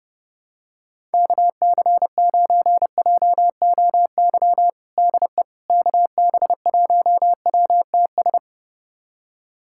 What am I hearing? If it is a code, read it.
Morse code, heard as a keyed tone, about 32 wpm, 700 Hz. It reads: KC9JOY DE KB1WTS
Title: USA Callsign DE Callsign - 6 Characters - 30wpm